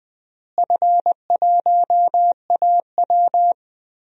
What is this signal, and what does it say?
Morse code, heard as a keyed tone, about 21 wpm, 700 Hz. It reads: F1AW
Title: DXCC-100 Call Signs - 4 Characters - Repeated 2x - 20wpm